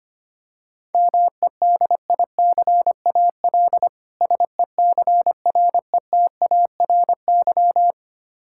Morse code, heard as a keyed tone, about 25 wpm, 700 Hz. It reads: MEDICAL SECRETARY
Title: K1USN SST QSO Practice 25wpm